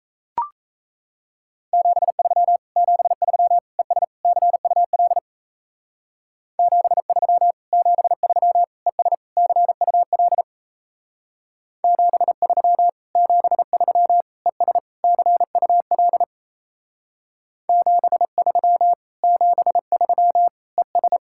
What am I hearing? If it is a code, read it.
Morse code, heard as a keyed tone, about 34 wpm, 700 Hz. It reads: 73 73 ES CUL 73 73 ES CUL 73 73 ES CUL 73 73 ES
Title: Speed-Racing Common QSO Phrases Encore - 28wpm